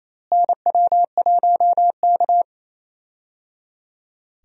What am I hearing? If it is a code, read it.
Morse code, heard as a keyed tone, about 28 wpm, 700 Hz. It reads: NW1K
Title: USA Call Signs - 4 Characters - Repeated 3x - 28wpm